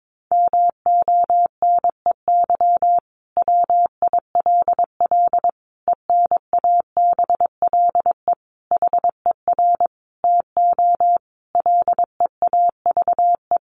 Morse code, heard as a keyed tone, about 22 wpm, 700 Hz. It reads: MONEY WILL ENABLE HER TO LEAVE
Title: Sentences from Top 1500 Words 22wpm